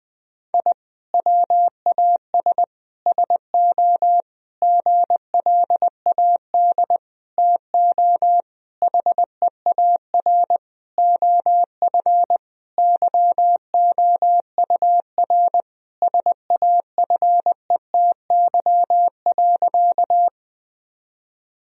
Morse code, heard as a keyed tone, about 20 wpm, 700 Hz. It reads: I WAS SO GLAD TO HEAR OF YOUR SAFETY.
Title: Sentences from Top 1000 Words - Review Part 4 - 20wpm